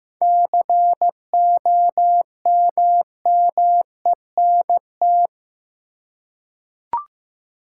Morse code, heard as a keyed tone, about 15 wpm, 700 Hz. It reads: COMMENT
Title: Seven Letter Words - 15wpm